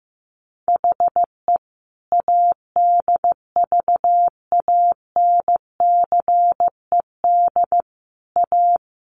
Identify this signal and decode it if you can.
Morse code, heard as a keyed tone, about 15 wpm, 700 Hz. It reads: HE ADVANCED A
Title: Sentences from Top 1200 Words 15wpm